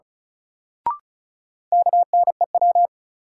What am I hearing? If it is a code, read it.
Morse code, heard as a keyed tone, about 36 wpm, 700 Hz. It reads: KNEW
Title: Sentences from Top 400 Words - New Word Drill - 35wpm